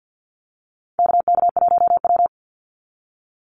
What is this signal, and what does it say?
Morse code, heard as a keyed tone, about 48 wpm, 700 Hz. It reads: KK1W
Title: NA SOTA Activator Callsigns 50wpm